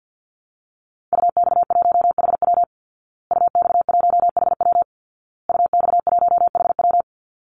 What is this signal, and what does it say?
Morse code, heard as a keyed tone, about 50 wpm, 700 Hz. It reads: UX1HW UX1HW UX1HW
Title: Call Signs Repeated 2x - 50wpm